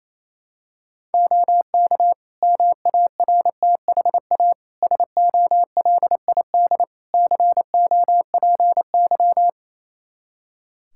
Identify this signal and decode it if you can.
Morse code, heard as a keyed tone, about 28 wpm, 700 Hz. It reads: OK MARTHA SOLID COPY
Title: Practice QSO - Exchange 3 - More Info - 28wpm